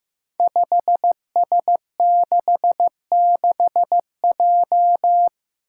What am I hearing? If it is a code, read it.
Morse code, heard as a keyed tone, about 15 wpm, 700 Hz. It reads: 5S66J
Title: Five Challenging Letter-Number Combinations 15wpm